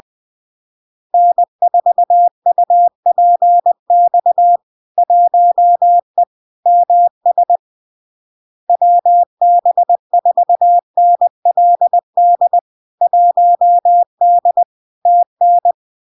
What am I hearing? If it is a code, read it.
Morse code, heard as a keyed tone, about 20 wpm, 700 Hz. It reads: N4UPX 1E MS WB4NLD 1D TN
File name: LARK-FD-2022-Log-in-CW-20wpm-700-Hz